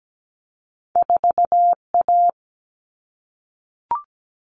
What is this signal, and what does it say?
Morse code, heard as a keyed tone, about 17 wpm, 700 Hz. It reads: 4A
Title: All Cut Numbers - 2-Digit 17wpm